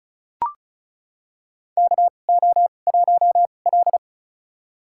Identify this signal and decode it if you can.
Morse code, heard as a keyed tone, about 34 wpm, spaced as for 25 wpm, 700 Hz. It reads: KO1L
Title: USA Call Signs - 4 Characters - 2x Character Spacing - 35wpm